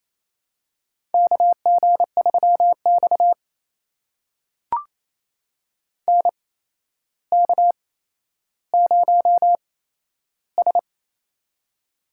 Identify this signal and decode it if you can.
Morse code, heard as a keyed tone, about 28 wpm, spaced as for 12 wpm, 700 Hz. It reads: KG3X NK0S
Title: USA Call Signs - 4 Characters - 8x Character Spacing - 28wpm